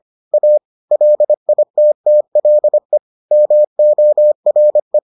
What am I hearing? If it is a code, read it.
Morse code, heard as a keyed tone, about 25 wpm, 600 Hz. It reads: A LITTLE MORE